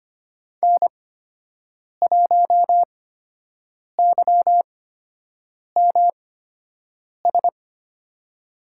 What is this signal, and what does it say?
Morse code, heard as a keyed tone, about 25 wpm, spaced as for 7 wpm, 700 Hz. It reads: N1YMS
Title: USA Call Signs - 5 Characters - 8x Character Spacing - 25wpm